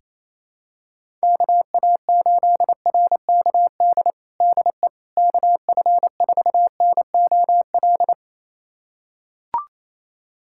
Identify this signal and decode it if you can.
Morse code, heard as a keyed tone, about 27 wpm, 700 Hz. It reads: KA8RKD DE KF4NOL
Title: USA Callsign DE Callsign - 6 Characters - 28wpm